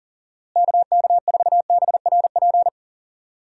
Morse code, heard as a keyed tone, about 37 wpm, 700 Hz. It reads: KK4BRP
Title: USA Call Signs - 6 Characters - Repeated 2x - 40wpm